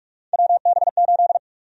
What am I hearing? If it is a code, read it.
Morse code, heard as a keyed tone, about 42 wpm, 700 Hz. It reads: WB8
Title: USA Call Sign Prefixes - Repeated 2x - 45wpm